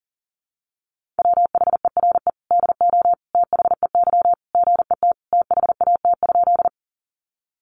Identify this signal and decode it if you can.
Morse code, heard as a keyed tone, about 40 wpm, 700 Hz. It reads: WHERE DO THEY GET THAT?
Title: Sentences from Top 200 Words 40wpm